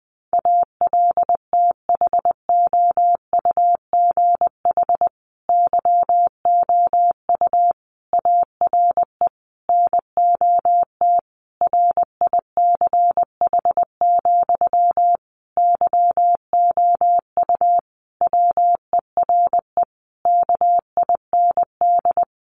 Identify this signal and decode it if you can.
Morse code, heard as a keyed tone, about 20 wpm, 700 Hz. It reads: ALTHOUGH YOU ARE NOT RICH, YOU WERE KIND
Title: Sentences from Top 1100 Words 20wpm